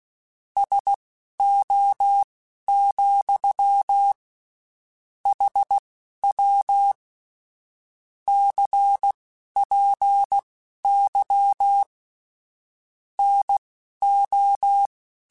Morse code, heard as a keyed tone, about 16 wpm, spaced as for 11 wpm, 800 Hz. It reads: SO, HW CPY NO
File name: QSOs / QSOs-mp3s / WD0FBT 1@15